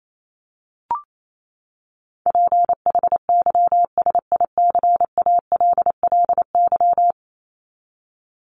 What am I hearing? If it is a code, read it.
Morse code, heard as a keyed tone, about 28 wpm, 700 Hz. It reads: PHYSICALLY
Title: Top 1501-1750 Words 28wpm